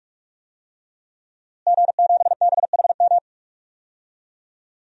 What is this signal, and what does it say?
Morse code, heard as a keyed tone, about 46 wpm, 700 Hz. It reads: G7BHM
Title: DXCC-100 Call Signs - 5 Characters - Repeated 3x - 45wpm